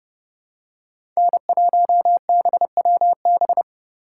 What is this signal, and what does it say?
Morse code, heard as a keyed tone, about 30 wpm, 700 Hz. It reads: N1BWB
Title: USA Call Signs - 5 Characters - 2x Character Spacing - 30wpm